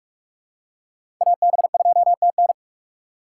Morse code, heard as a keyed tone, about 46 wpm, 700 Hz. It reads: AB2TN
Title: USA Call Signs - 5 Characters - 4x Character Spacing - 45wpm